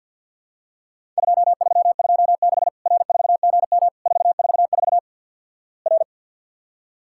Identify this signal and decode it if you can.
Morse code, heard as a keyed tone, about 48 wpm, 700 Hz. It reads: 2326 R4GM 444 R
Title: Worked All Europe DX Contest - QTC Exchange with Repeats - 50wpm